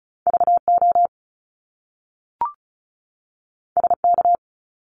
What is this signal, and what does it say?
Morse code, heard as a keyed tone, about 36 wpm, 700 Hz. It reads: VO SK
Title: Top Two Letter Combinations in Words 35wpm